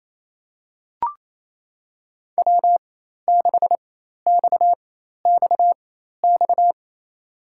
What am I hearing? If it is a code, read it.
Morse code, heard as a keyed tone, about 28 wpm, spaced as for 13 wpm, 700 Hz. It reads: W6XXX
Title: USA Call Signs - 5 Characters - 4x Character Spacing - 28wpm